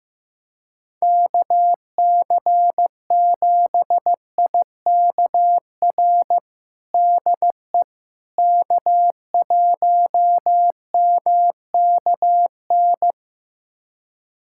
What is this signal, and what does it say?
Morse code, heard as a keyed tone, about 15 wpm, 700 Hz. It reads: KC7IKR DE K1MKN
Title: USA Callsign DE Callsign 15wpm